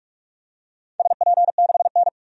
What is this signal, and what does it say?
Morse code, heard as a keyed tone, about 42 wpm, 700 Hz. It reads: SP6N
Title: DXCC-100 Call Signs - 4 Characters - 45wpm